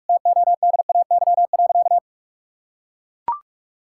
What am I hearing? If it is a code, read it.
Morse code, heard as a keyed tone, about 41 wpm, 700 Hz. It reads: TODAY.
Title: Sentences from Top 1000 Words - Review Part 6 - 45wpm